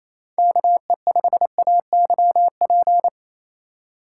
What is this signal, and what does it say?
Morse code, heard as a keyed tone, about 28 wpm, 700 Hz. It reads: KE5AYP
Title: USA Call Signs - 6 Characters - 3x Character Spacing - 28wpm